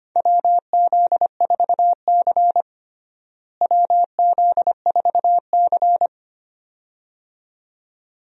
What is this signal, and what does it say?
Morse code, heard as a keyed tone, about 25 wpm, 700 Hz. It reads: WZ4C WZ4C
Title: USA Call Signs - 4 Characters - Repeated 2x - 25wpm